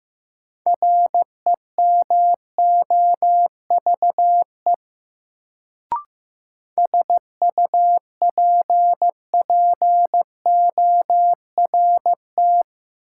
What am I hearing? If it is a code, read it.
Morse code, heard as a keyed tone, about 15 wpm, 700 Hz. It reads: REMOVE SUPPORT
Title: Sentences from Top 800 Words - New Word Drill - 15wpm